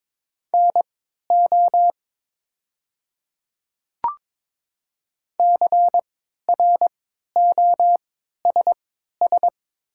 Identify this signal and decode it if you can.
Morse code, heard as a keyed tone, about 22 wpm, spaced as for 13 wpm, 700 Hz. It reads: NO CROSS
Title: 3x Character Spacing - Top 300 Words - 22wpm